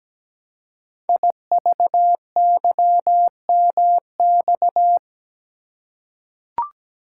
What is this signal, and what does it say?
Morse code, heard as a keyed tone, about 17 wpm, 700 Hz. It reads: IVYMX
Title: Code Groups - 5 Letters - 17wpm